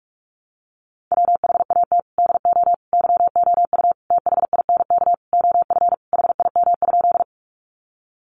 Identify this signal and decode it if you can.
Morse code, heard as a keyed tone, about 44 wpm, 700 Hz. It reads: WHAT DO YOU THINK OF HIM?
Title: Sentences from Top 200 Words 45wpm